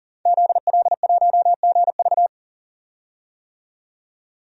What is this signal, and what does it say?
Morse code, heard as a keyed tone, about 40 wpm, 700 Hz. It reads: ZL1GV
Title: DXCC-100 Call Signs - 5 Characters - 40wpm